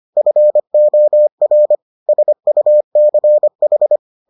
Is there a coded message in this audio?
Yes: Morse code from 0.2 to 4.0 s, heard as a keyed tone, about 25 wpm, 600 Hz. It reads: FOR SUCH